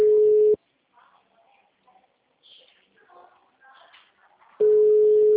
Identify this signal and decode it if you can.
Ringback tone (Europe, 425 Hz).
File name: out- -2003